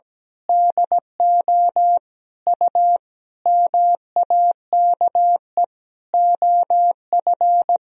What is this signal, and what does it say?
Morse code, heard as a keyed tone, about 17 wpm, 700 Hz. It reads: DO U MAKE OF
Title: Sentences from Top 100 Words and Abbreviations 17wpm